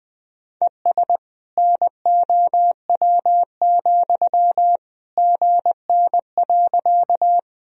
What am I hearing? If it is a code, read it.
Morse code, heard as a keyed tone, about 20 wpm, 700 Hz. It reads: ES NOW, GN.
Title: Sentences from Top 300 Words and Abbreviations 20wpm